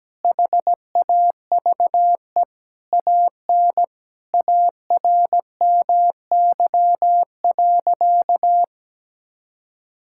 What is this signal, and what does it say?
Morse code, heard as a keyed tone, about 17 wpm, 700 Hz. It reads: HAVE AN ARMY.
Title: Sentences from Top 1000 Words - Review Part 5 - 17wpm